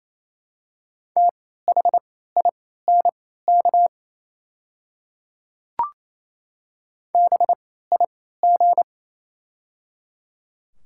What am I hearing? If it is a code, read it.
Morse code, heard as a keyed tone, about 27 wpm, spaced as for 16 wpm, 700 Hz. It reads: THINK BIG
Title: 3x Character Spacing - Top 300 Words - 28wpm